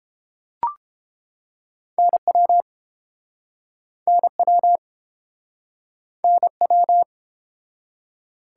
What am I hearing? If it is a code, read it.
Morse code, heard as a keyed tone, about 31 wpm, 700 Hz. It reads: NW NW NW